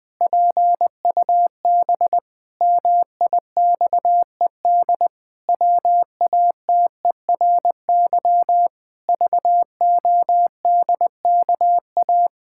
Morse code, heard as a keyed tone, about 20 wpm, 700 Hz. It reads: PUB MIXED WATERY VODKA
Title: Pangram Challenge v2 - 20wpm